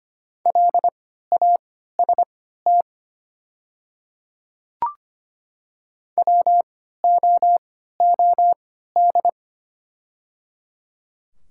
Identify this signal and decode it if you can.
Morse code, heard as a keyed tone, about 25 wpm, spaced as for 14 wpm, 690 Hz. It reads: LAST WOOD
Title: 3x Character Spacing - Top 300 Words - 25wpm